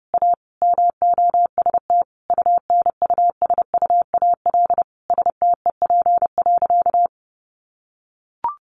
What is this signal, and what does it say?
Morse code, heard as a keyed tone, about 30 wpm, 700 Hz. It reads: A MOST UNUSUAL STEP.
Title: Sentences from Top 1400 Words 30wpm